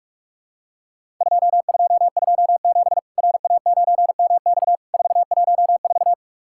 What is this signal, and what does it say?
Morse code, heard as a keyed tone, about 44 wpm, 700 Hz. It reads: 2227 RA9MX 414
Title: Worked All Europe DX Contest - QTC Exchange - 45wpm